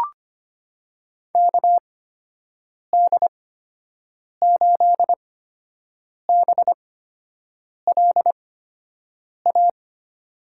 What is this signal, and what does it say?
Morse code, heard as a keyed tone, about 25 wpm, spaced as for 7 wpm, 700 Hz. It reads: KD8BLA
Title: USA Call Signs - 6 Characters - 8x Character Spacing - 25wpm